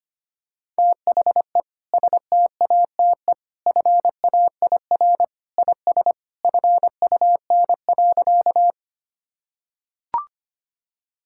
Morse code, heard as a keyed tone, about 25 wpm, 700 Hz. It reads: THE STATE FAIR IS FUN.
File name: practice / 25wpm / Sentences with 2-Word Phrases - 25wpm